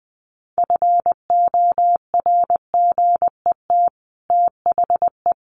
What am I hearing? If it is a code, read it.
Morse code, heard as a keyed tone, about 20 wpm, 700 Hz. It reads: FORGET THE
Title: Sentences from Top 1200 Words 20wpm